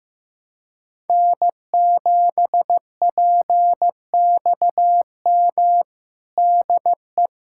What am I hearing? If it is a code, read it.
Morse code, heard as a keyed tone, about 15 wpm, 700 Hz. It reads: N7PXM DE
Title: USA Callsign DE Callsign - 5 Characters - 15wpm